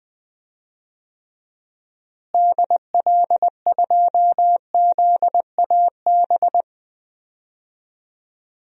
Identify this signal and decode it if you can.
Morse code, heard as a keyed tone, about 20 wpm, 700 Hz. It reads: DL2ZAB
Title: DXCC-100 Call Signs - 6 Characters - Repeated 3x - 20wpm